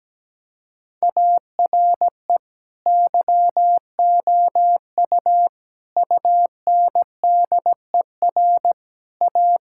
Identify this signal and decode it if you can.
Morse code, heard as a keyed tone, about 17 wpm, 700 Hz. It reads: ARE YOU UNDER A
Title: Sentences from Top 1000 Words - Review Part 6 - 17wpm